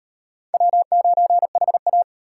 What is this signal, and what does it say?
Morse code, heard as a keyed tone, about 38 wpm, 700 Hz. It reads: W9HA